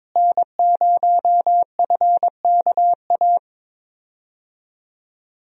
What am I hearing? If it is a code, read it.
Morse code, heard as a keyed tone, about 22 wpm, 700 Hz. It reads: N0FKA
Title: USA Call Signs - 5 Characters - Repeated 3x - 22wpm